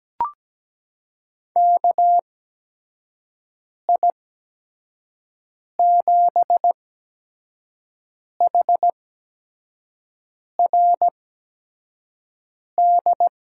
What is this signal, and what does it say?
Morse code, heard as a keyed tone, about 17 wpm, spaced as for 5 wpm, 700 Hz. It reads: KI7HRD
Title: USA Call Signs - 6 Characters - 8x Character Spacing - 17wpm